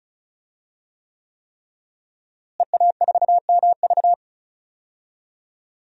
Morse code, heard as a keyed tone, about 36 wpm, 700 Hz. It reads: EA4MV
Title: DXCC-100 Call Signs - 5 Characters - Repeated 3x - 35wpm